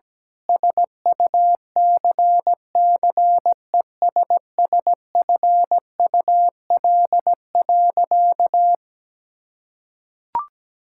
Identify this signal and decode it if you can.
Morse code, heard as a keyed tone, about 17 wpm, 700 Hz. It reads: SUCCESSFUL.
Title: Sentences from Top 1600 Words 17wpm